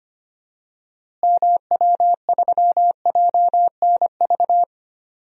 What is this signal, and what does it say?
Morse code, heard as a keyed tone, about 25 wpm, 700 Hz. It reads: MW3JNV
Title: DXCC-100 Call Signs - 6 Characters - Repeated 3x - 25wpm